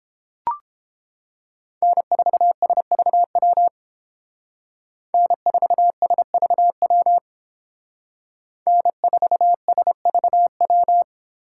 Morse code, heard as a keyed tone, about 30 wpm, 700 Hz. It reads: N4SVW N4SVW N4SVW